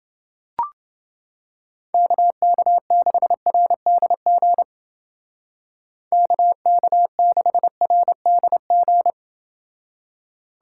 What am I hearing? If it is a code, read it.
Morse code, heard as a keyed tone, about 29 wpm, 700 Hz. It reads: KK6RDG KK6RDG